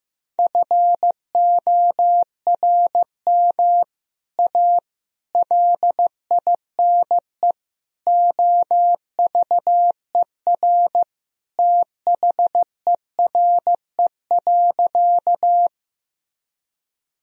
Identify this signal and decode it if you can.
Morse code, heard as a keyed tone, about 15 wpm, 700 Hz. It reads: FORM A LINE OVER THERE.
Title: Sentences from Top 1000 Words - Review Part 1 - 15wpm